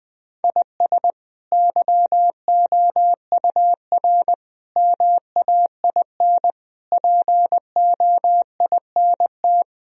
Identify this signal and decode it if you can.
Morse code, heard as a keyed tone, about 20 wpm, 700 Hz. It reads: IS YOUR MAIN POINT
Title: Sentences from Top 1000 Words - Review Part 2 - 20wpm